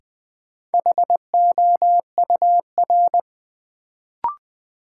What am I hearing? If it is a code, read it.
Morse code, heard as a keyed tone, about 20 wpm, 700 Hz. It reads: HOUR
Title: Sentences from Top 400 Words - New Word Drill - 20wpm